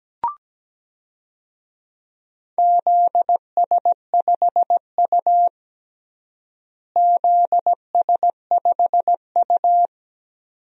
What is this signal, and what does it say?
Morse code, heard as a keyed tone, about 17 wpm, 700 Hz. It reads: ZS5U ZS5U
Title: DXCC-100 Call Signs - 4 Characters - Repeated 2x - 17wpm